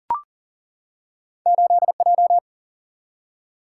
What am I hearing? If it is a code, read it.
Morse code, heard as a keyed tone, about 41 wpm, 700 Hz. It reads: 8J
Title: Single-Letter Single-Number 40wpm